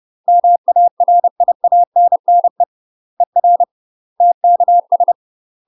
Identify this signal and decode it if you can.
Morse code, heard as a keyed tone, about 30 wpm, 700 Hz. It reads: MARIANNE ER TKS
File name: UK MASTER QSOs TRAINING / 30WPM / UK MASTER QSOs 0039